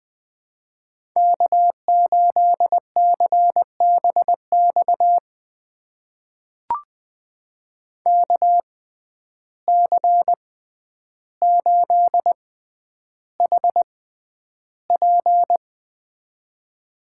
Morse code, heard as a keyed tone, about 20 wpm, spaced as for 10 wpm, 700 Hz. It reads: K8CBX KC8HP
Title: USA Call Signs - 5 Characters - 6x Character Spacing - 20wpm